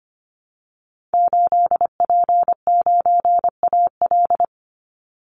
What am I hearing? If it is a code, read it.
Morse code, heard as a keyed tone, about 25 wpm, 700 Hz. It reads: 8P9AL
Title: Call Signs Repeated 2x - 25wpm